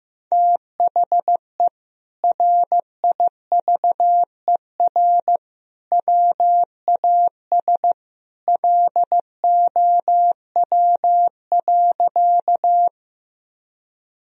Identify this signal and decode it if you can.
Morse code, heard as a keyed tone, about 15 wpm, 700 Hz. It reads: THE RIVER WAS LOW.
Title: Sentences from Top 1000 Words - Review Part 1 - 15wpm